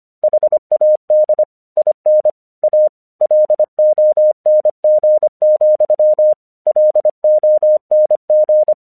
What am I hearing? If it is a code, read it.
Morse code, heard as a keyed tone, about 25 wpm, 610 Hz. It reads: HAD IN A LONG, LONG
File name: 1 English 25wpm / AQC36